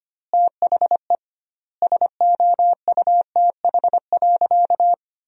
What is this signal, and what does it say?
Morse code, heard as a keyed tone, about 25 wpm, 700 Hz. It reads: THE SOUTH.
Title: 2x Word-Spacing - Sentences from Top 400 Words - 25wpm